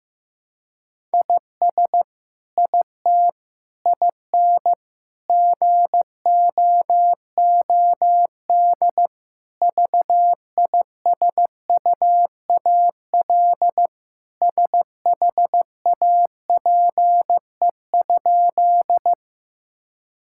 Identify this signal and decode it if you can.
Morse code, heard as a keyed tone, about 15 wpm, 700 Hz. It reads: IS IT IN GOOD VISUAL SHAPE?
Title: Sentences from Top 1000 Words - Review Part 6 - 15wpm